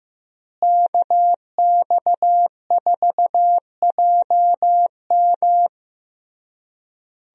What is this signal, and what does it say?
Morse code, heard as a keyed tone, about 15 wpm, 700 Hz. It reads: KX4JM
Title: USA Call Signs - 5 Characters - Repeated 3x - 15wpm